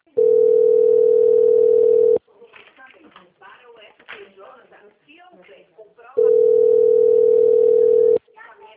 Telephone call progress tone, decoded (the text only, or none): ringback tone